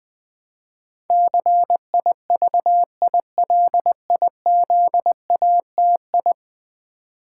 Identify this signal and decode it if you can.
Morse code, heard as a keyed tone, about 20 wpm, 700 Hz. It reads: CIVILIZATI
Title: Build A Word - 5 Letters At A Time - 12-Letter Words - 20wpm